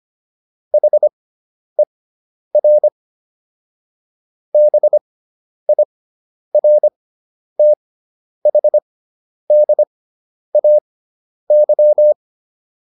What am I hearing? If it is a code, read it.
Morse code, heard as a keyed tone, about 25 wpm, spaced as for 10 wpm, 600 Hz. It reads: HER BIRTHDAY